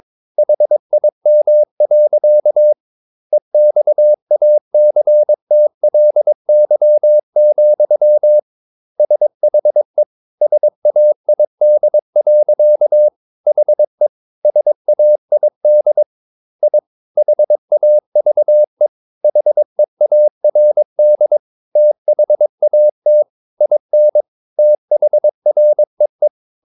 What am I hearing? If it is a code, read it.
Morse code, heard as a keyed tone, about 22 wpm, 600 Hz. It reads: HIM. EXACTLY, SHE SAID. HE SAID I HAVE HEARD THAT IN THREE
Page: Romanen Never af Ken Follet på engelsk.